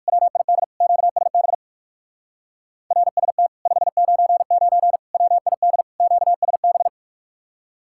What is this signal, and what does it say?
Morse code, heard as a keyed tone, about 69 wpm, spaced as for 58 wpm, 700 Hz. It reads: WID QSB RST 599 WID QSB